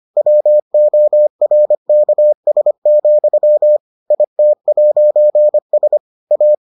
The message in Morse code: WORKS, IT'S A

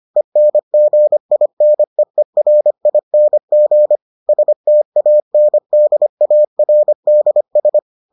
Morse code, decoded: ENGINEERING STANDARDS

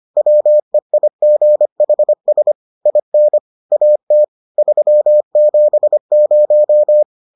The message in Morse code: WEIGHS IN AT 370

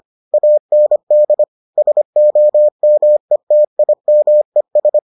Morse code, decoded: AND SOMETIMES

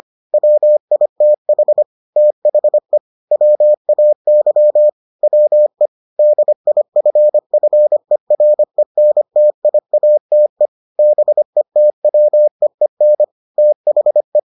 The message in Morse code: WITH THE WAY WE DIFFERENTIATE BETWEEN THE

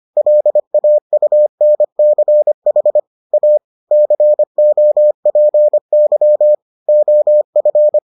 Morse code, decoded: LAUNCH A COPY OF

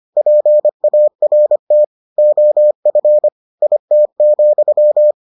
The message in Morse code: PART OF IT,

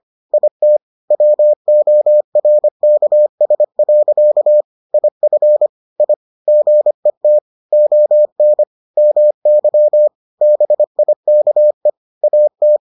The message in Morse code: IT WORKS. IF I GET ON MY BIKE AT

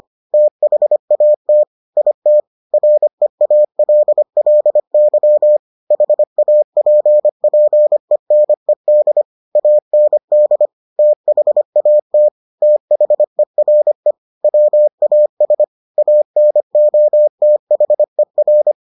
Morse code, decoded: THAT IT REALLY HAPPENED AND THAT THERE WAS ANOTHER

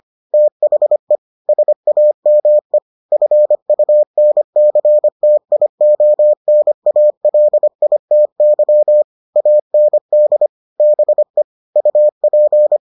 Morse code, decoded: THE SAME FUNCTIONALITY AND BE UP

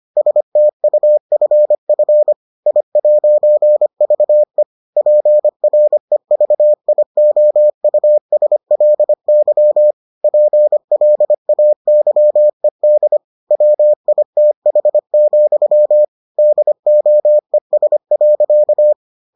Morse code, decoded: STUFF I'VE PREVIOUSLY PLAYED WITH, DOES.

